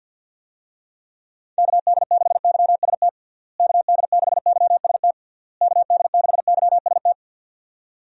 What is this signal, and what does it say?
Morse code, heard as a keyed tone, about 50 wpm, 700 Hz. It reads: KD6YST KD6YST KD6YST
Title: USA Call Signs - 6 Characters - Repeated 3x - 50wpm